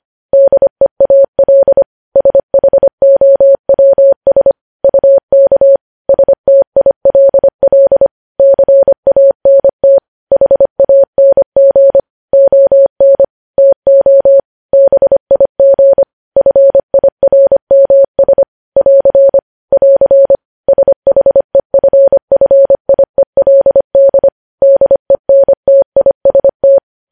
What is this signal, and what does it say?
Morse code, heard as a keyed tone, about 25 wpm, 550 Hz. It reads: DEAL SHOWS UK STILL CANT HANG ON TO BIG FIRMS <AR> <AR> SHEFFIELD DENTIST
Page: news headlines in morse code at 25 wpm